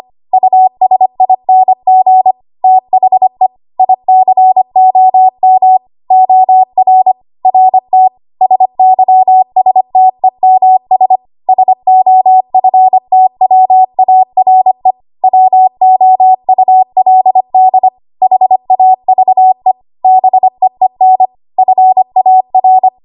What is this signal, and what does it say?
Morse code, heard as a keyed tone, about 25 wpm, 750 Hz. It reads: USING THE ICOM OR RT SYSTEMS SOFTWARE WOULD HAVE BEEN FAR
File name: Archive / 25 WPM / 25WPM